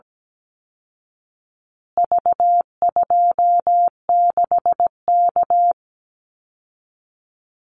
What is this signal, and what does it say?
Morse code, heard as a keyed tone, about 17 wpm, 700 Hz. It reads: V26K
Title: Call Signs Repeated 2x - 17wpm